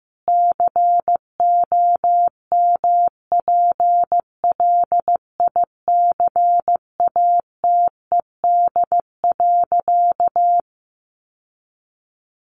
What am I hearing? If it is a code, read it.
Morse code, heard as a keyed tone, about 14 wpm, 700 Hz. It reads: COMPLICATED.
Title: Sentences from Top 1200 Words 15wpm